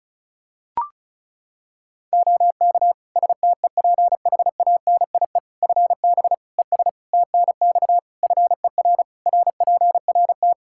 Morse code, heard as a keyed tone, about 34 wpm, 700 Hz. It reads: OK STEPHANIE FB ES TNX FER RPRT
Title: Practice QSO - Exchange 2 - Info - 35wpm